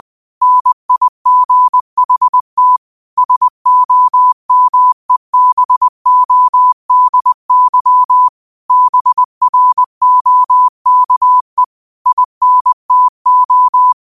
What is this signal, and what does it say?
Morse code, heard as a keyed tone, about 20 wpm, 1000 Hz. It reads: NIGHT SOMEBODY BROKE INTO
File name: 2024-09-13-20wpm